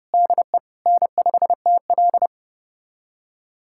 Morse code, heard as a keyed tone, about 30 wpm, 700 Hz. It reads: DE N5TL
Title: USA Callsign DE Callsign - 4 Characters - 30wpm